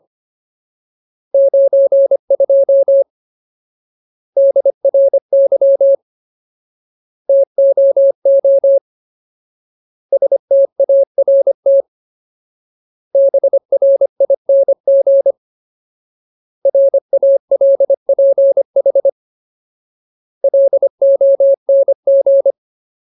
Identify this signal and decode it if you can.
Morse code, heard as a keyed tone, about 25 wpm, 550 Hz. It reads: 92 DRY TOO START BRING RALPH LONG
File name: Ultimate Word List / 25WPM Ultimate